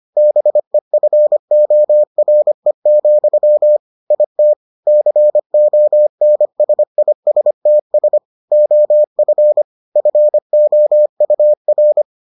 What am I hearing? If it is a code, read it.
Morse code, heard as a keyed tone, about 25 wpm, 600 Hz. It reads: BEFORE, IT CONSISTS OF FOUR